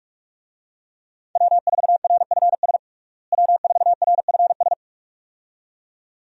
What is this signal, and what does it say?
Morse code, heard as a keyed tone, about 46 wpm, 700 Hz. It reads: W4RFS W4RFS
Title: USA Call Signs - 5 Characters - Repeated 2x - 45wpm